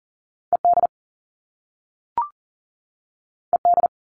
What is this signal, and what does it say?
Morse code, heard as a keyed tone, about 41 wpm, 700 Hz. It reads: ED ED
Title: All Cut Numbers - 2-Digit 40wpm